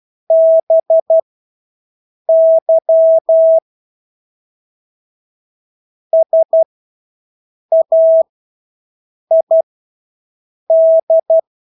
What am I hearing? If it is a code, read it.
Morse code, heard as a keyed tone, about 12 wpm, spaced as for 6 wpm, 650 Hz. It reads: BY SAID
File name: Engelske ord 12-6 wpm